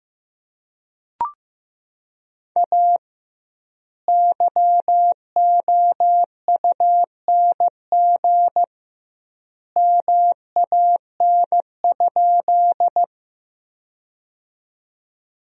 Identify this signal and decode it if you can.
Morse code, heard as a keyed tone, about 15 wpm, 700 Hz. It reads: A YOUNG MAN?
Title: 2x Word-Spacing - Sentences from Top 400 Words - 15wpm